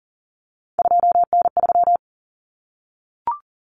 Morse code, heard as a keyed tone, about 40 wpm, 700 Hz. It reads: 2N3
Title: Common Cut Numbers - 3-Digit 40wpm